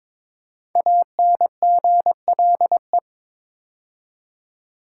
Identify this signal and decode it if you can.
Morse code, heard as a keyed tone, about 22 wpm, 700 Hz. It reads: ANGLE
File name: 5 Letter Words - First 3 Letters Given - 22wpm